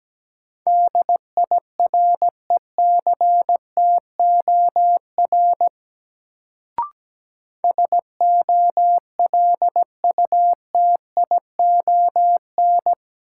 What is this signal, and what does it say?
Morse code, heard as a keyed tone, about 17 wpm, 700 Hz. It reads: DIRECTOR SOLUTION
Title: Sentences from Top 1300 Words - New Word Drill Alt - 17wpm